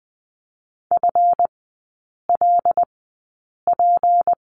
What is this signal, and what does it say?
Morse code, heard as a keyed tone, about 20 wpm, spaced as for 8 wpm, 700 Hz. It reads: FLP
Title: 152 Lesson 25e - Letter p - Three Characters 20wpm